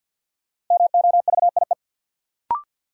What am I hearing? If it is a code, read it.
Morse code, heard as a keyed tone, about 52 wpm, 700 Hz. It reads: MOVIE
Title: Sentences from Top 1000 Words - Review Part 5 - New Word Drill - 50wpm